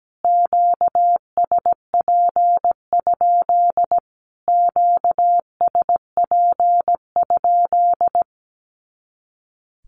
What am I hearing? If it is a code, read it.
Morse code, heard as a keyed tone, about 17 wpm, 700 Hz. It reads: QSP? QSP?
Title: Q-Codes Repeated 3x 17wpm